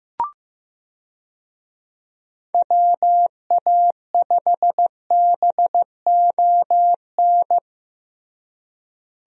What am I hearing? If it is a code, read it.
Morse code, heard as a keyed tone, about 15 wpm, 700 Hz. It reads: WA5BON
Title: USA Call Signs - 6 Characters - Repeated 2x - 15wpm